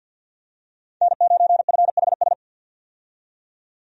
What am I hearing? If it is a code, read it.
Morse code, heard as a keyed tone, about 50 wpm, 700 Hz. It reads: N9FHS
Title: USA Call Signs - 5 Characters - 50wpm